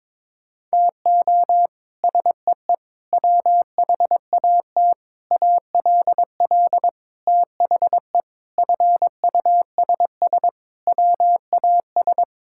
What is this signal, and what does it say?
Morse code, heard as a keyed tone, about 22 wpm, 700 Hz. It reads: TO SEE WHAT ALL THE FUSS WAS
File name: practice / 22wpm / Sets of 8 Words - 22wpm